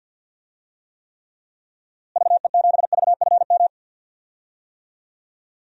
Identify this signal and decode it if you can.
Morse code, heard as a keyed tone, about 47 wpm, 700 Hz. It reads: VE7VFM
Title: DXCC-100 Call Signs - 6 Characters - Repeated 3x - 50wpm